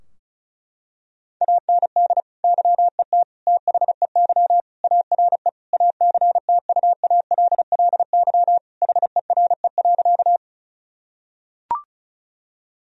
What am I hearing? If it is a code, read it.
Morse code, heard as a keyed tone, about 36 wpm, 700 Hz. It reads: AND YET THEY ARE ACTUALLY HERE.
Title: Sentences from Top 1000 Words - Review Part 4 - 35wpm